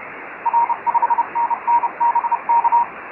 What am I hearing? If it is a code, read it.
Morse code, heard as a keyed tone, about 30 wpm, 940 Hz, with radio noise. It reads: R5NN BK
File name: FO29 JG8NQJ